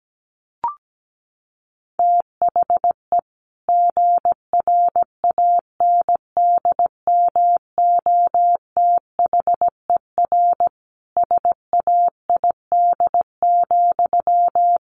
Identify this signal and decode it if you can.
Morse code, heard as a keyed tone, about 17 wpm, 700 Hz. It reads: THE GRANDMOTHER SAID,
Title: Sentences from Top 1200 Words 17wpm